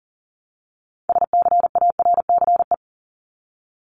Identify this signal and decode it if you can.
Morse code, heard as a keyed tone, about 40 wpm, 700 Hz. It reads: SCARCE
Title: LY Suffix - Top 100 Words with Base Words 40wpm